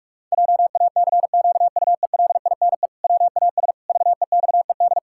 Morse code, heard as a keyed tone, about 46 wpm, 700 Hz. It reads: JACQUELINE WAS VEXED
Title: Pangram Challenge v2 - 45wpm